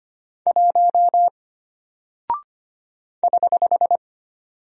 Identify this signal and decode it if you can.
Morse code, heard as a keyed tone, about 25 wpm, 700 Hz. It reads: 1 <HH>
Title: Single Character (Letter:Number:Punctuation:Prosign) Alt 25wpm